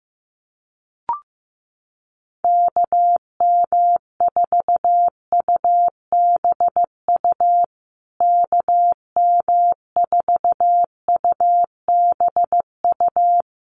Morse code, heard as a keyed tone, about 14 wpm, 700 Hz. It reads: KM4UBU KM4UBU
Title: USA Call Signs Repeated 3x 15wpm